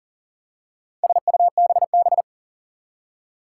Morse code, heard as a keyed tone, about 40 wpm, 690 Hz. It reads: SUBB
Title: Four Challenging Letter-Number Combinations 40wpm